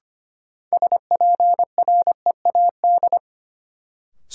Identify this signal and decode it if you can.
Morse code, heard as a keyed tone, about 25 wpm, 700 Hz. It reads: SPREAD
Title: Sentences from Top 1200 Words - New Word Drill - 25wpm